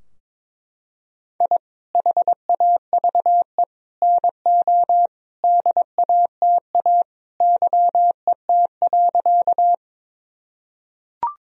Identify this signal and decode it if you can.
Morse code, heard as a keyed tone, about 22 wpm, 700 Hz. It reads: I HAVE NO DATA YET.
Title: Sentences from Top 1000 Words - Review Part 3 - 22wpm